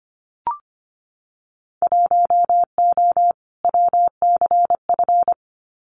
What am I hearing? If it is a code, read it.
Morse code, heard as a keyed tone, about 25 wpm, 690 Hz. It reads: 1O WCF
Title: Winter Field Day Exchanges 25wpm